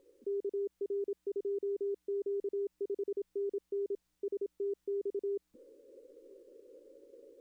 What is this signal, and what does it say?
Morse code, heard as a keyed tone, about 27 wpm, 400 Hz. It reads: KR2Q 5NN STX
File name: contests / arrl-160-k5tr-op-w5tn -0200